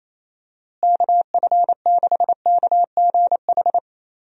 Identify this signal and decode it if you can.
Morse code, heard as a keyed tone, about 28 wpm, 700 Hz. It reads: KF6KGH
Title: USA Call Signs - 6 Characters - Repeated 3x - 28wpm